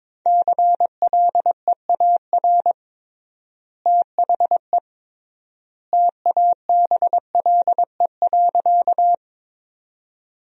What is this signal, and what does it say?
Morse code, heard as a keyed tone, about 22 wpm, 700 Hz. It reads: CLEAR THE TABLE.
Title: 3x Word-Spacing - Sentences from Top 500 Words - 22wpm